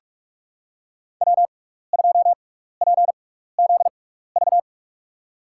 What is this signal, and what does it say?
Morse code, heard as a keyed tone, about 44 wpm, spaced as for 15 wpm, 700 Hz. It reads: W2PZV
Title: USA Call Signs - 5 Characters - 6x Character Spacing - 45wpm